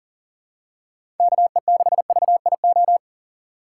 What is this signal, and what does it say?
Morse code, heard as a keyed tone, about 40 wpm, 700 Hz. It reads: KE6VIO
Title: USA Call Signs - 6 Characters - 3x Character Spacing - 40wpm